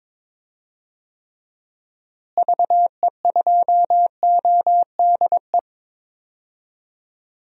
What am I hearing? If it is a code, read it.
Morse code, heard as a keyed tone, about 21 wpm, 700 Hz. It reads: VE2ODE
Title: DXCC-100 Call Signs - 6 Characters - Repeated 2x - 22wpm